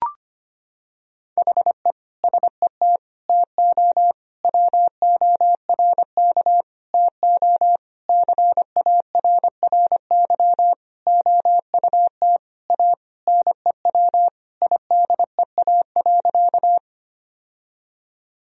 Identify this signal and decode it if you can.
Morse code, heard as a keyed tone, about 25 wpm, 700 Hz. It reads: HE SET TO WORK TO CARRY OUT A NEW IDEA.
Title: Sentences from Top 1000 Words - Review Part 1 - 25wpm